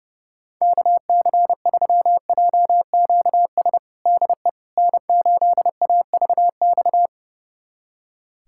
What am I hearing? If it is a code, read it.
Morse code, heard as a keyed tone, about 32 wpm, 700 Hz. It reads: KC3JQS DE N8AVX
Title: USA Callsign DE Callsign 30wpm